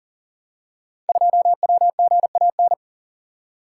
Morse code, heard as a keyed tone, about 42 wpm, 700 Hz. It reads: 2WGAN
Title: Code Groups - 5 Letter:Numbers - 40wpm